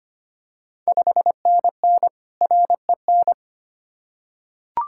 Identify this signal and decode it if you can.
Morse code, heard as a keyed tone, about 25 wpm, 700 Hz. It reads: 5NN REN
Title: Ontario QSO Party Exchange 25wpm